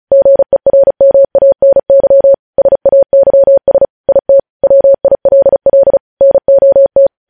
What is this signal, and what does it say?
Morse code, heard as a keyed tone, about 36 wpm, 550 Hz. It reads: GERMANY SAYS IT WILL NOT
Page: News headlines in Morse code at 35 wpm